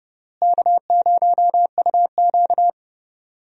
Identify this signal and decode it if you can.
Morse code, heard as a keyed tone, about 31 wpm, 700 Hz. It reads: K0UQ
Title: USA Call Signs - 4 Characters - Repeated 3x - 30wpm